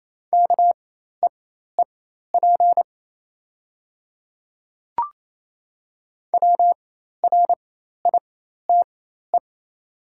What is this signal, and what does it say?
Morse code, heard as a keyed tone, about 28 wpm, spaced as for 13 wpm, 700 Hz. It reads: KEEP WRITE
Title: 4x Character Spacing - Top 500 Words - 28wpm